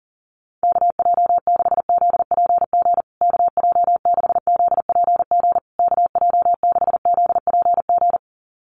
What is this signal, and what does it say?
Morse code, heard as a keyed tone, about 40 wpm, 700 Hz. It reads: KJ6ZPG KJ6ZPG KJ6ZPG
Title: USA Call Signs Repeated 3x 40wpm